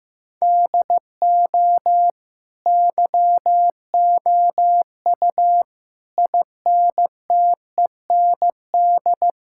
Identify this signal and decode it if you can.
Morse code, heard as a keyed tone, about 15 wpm, 700 Hz. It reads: DO YOU INTEND
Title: Sentences from Top 1600 Words 15wpm